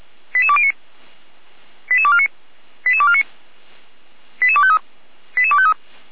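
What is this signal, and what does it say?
Five-tone selective call (ZVEI), heard as quick runs of five tones, tones of about 70 ms. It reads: 88209 88249 88249 88253 88253
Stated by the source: ZVEI five-tone selective call received by SDR, logged by Feuerwehr Orlamünde (fire brigade).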